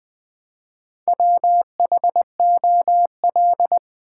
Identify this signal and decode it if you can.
Morse code, heard as a keyed tone, about 20 wpm, 700 Hz. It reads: WHOL
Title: Build A Word - 1 Letter At A Time - 3 to 5-Letter Words - 20wpm